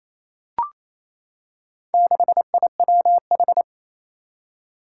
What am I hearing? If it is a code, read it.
Morse code, heard as a keyed tone, about 28 wpm, 700 Hz. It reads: 6IWH
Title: Four Challenging Letter-Number Combinations 28wpm